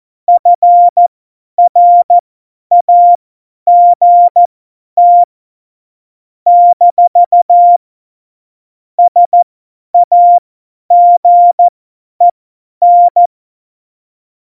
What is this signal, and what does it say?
Morse code, heard as a keyed tone, about 14 wpm, spaced as for 10 wpm, 700 Hz. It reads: FRAGT - SAGEN
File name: Morsekurs / Witze / Witz 09. Januar 2018, 50 F70